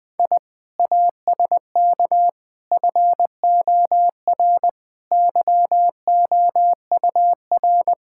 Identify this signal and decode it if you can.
Morse code, heard as a keyed tone, about 21 wpm, 700 Hz. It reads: I ASK FOR YOUR
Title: Sentences from Top 1000 Words - Review Part 5 - 20wpm